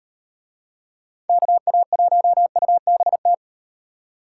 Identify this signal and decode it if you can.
Morse code, heard as a keyed tone, about 37 wpm, 700 Hz. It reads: KA1UBT